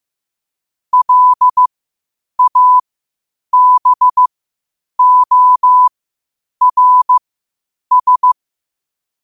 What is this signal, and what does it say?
Morse code, heard as a keyed tone, about 15 wpm, spaced as for 8 wpm, 1000 Hz. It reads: LABORS
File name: qotdfiles / 8wpm